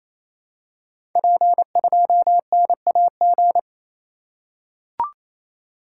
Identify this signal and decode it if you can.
Morse code, heard as a keyed tone, about 27 wpm, 700 Hz. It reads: P2NAG
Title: Code Groups - 5 Letter:Numbers - 28wpm